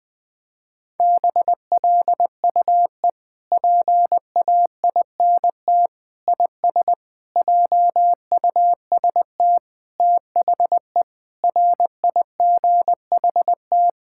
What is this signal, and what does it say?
Morse code, heard as a keyed tone, about 20 wpm, 700 Hz. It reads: BLUE PAINT IS JUST THE RIGHT
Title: Sentences from Top 1000 Words - Review Part 6 - 20wpm